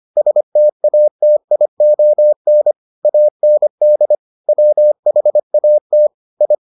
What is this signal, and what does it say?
Morse code, heard as a keyed tone, about 25 wpm, 600 Hz. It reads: STATION AND WHAT I